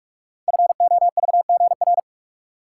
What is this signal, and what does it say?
Morse code, heard as a keyed tone, about 44 wpm, 700 Hz. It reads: FOVGR
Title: Code Groups - 5 Letters - 45wpm